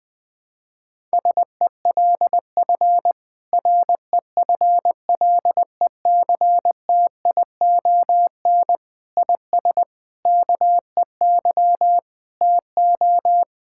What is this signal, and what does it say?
Morse code, heard as a keyed tone, about 20 wpm, 700 Hz. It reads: SELF REFLECTION IS KEY TO
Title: Difficult Sentences with Hints - 20wpm